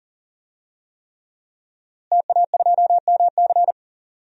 Morse code, heard as a keyed tone, about 40 wpm, 700 Hz. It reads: TA2MC
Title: DXCC-100 Call Signs - 5 Characters - Repeated 3x - 40wpm